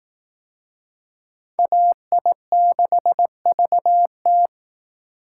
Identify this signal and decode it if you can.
Morse code, heard as a keyed tone, about 19 wpm, 700 Hz. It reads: AI6VT